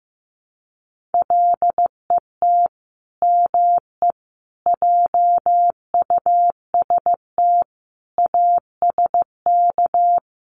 Morse code, heard as a keyed tone, about 15 wpm, 700 Hz. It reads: LET ME JUST ASK
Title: Sets of 5 Words Encore2 15wpm